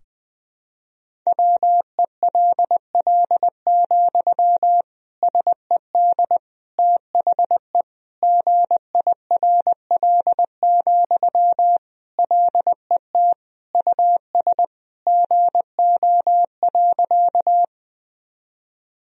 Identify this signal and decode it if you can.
Morse code, heard as a keyed tone, about 21 wpm, 700 Hz. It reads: WELL, SED THE GIRL, LET US GO.
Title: Sentences from Top 400 Words and Abbreviations 20wpm